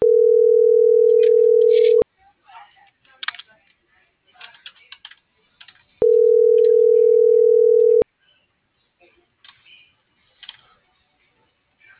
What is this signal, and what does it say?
Ringback tone (North America), 440 + 480 Hz.